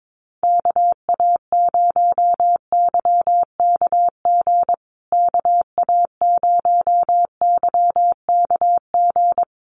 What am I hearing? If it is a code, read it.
Morse code, heard as a keyed tone, about 22 wpm, 700 Hz. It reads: KA0YKG KA0YKG
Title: USA Call Signs Repeated 3x 22wpm